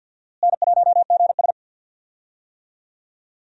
Morse code, heard as a keyed tone, about 49 wpm, 700 Hz. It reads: N1GS
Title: USA Call Signs - 4 Characters - Repeated 2x - 50wpm